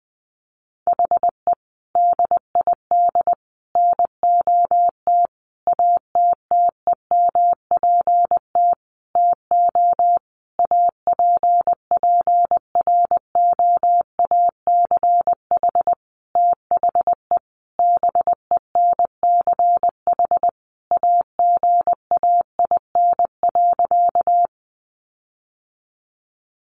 Morse code, heard as a keyed tone, about 20 wpm, 700 Hz. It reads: HE DID NOT ATTEMPT TO APPROACH THE BENCH AGAIN.
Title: Sentences from Top 1400 Words 20wpm